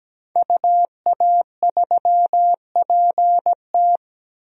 Morse code, heard as a keyed tone, about 17 wpm, 700 Hz. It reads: UA3PT
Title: DXCC-100 Call Signs - 5 Characters - 17wpm